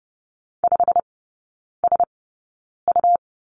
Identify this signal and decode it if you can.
Morse code, heard as a keyed tone, about 30 wpm, spaced as for 9 wpm, 700 Hz. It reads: 5SU
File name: Lesson 17e - Letter u - Three Characters 30wpm